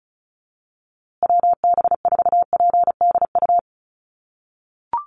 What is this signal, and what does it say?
Morse code, heard as a keyed tone, about 34 wpm, 700 Hz. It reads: WB4PDU
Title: USA Call Signs Repeated 2x 35wpm